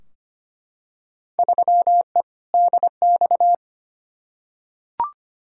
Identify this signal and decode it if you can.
Morse code, heard as a keyed tone, about 25 wpm, 700 Hz. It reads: 3E DX
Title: Field Day Exchanges Encore - 25wpm